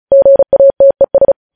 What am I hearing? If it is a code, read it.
Morse code, heard as a keyed tone, about 36 wpm, 550 Hz. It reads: GATES
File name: podcast / News headlines-1x-35x35